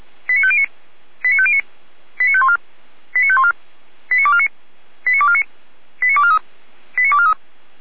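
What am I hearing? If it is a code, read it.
Five-tone selective call (ZVEI), heard as quick runs of five tones, tones of about 70 ms. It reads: 89509 89509 89514 89514 89249 89249 89243 89243
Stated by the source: Received on an SDR, ZVEI five-tone selective call, channel 455, from the feuerwehr-orlamuende fire brigade server.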